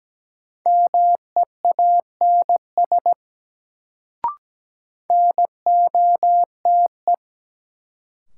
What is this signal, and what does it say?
Morse code, heard as a keyed tone, about 17 wpm, 700 Hz. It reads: MEANS NOTE
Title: Sets of 5 Words - New Word Drill - 17wpm